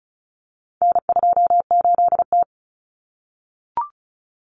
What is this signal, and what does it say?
Morse code, heard as a keyed tone, about 34 wpm, 700 Hz. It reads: N28T
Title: Common Cut Numbers - 4-Digit 35wpm